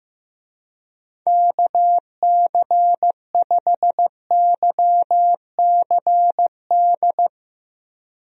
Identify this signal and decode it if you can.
Morse code, heard as a keyed tone, about 15 wpm, 700 Hz. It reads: KC5YCD
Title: USA Call Signs - 6 Characters - Repeated 2x - 15wpm